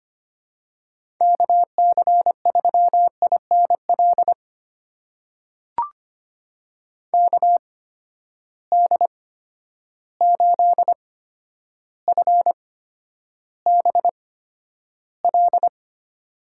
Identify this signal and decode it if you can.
Morse code, heard as a keyed tone, about 25 wpm, spaced as for 11 wpm, 700 Hz. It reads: KC3INL KD8FBL
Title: USA Call Signs - 6 Characters - 8x Character Spacing - 25wpm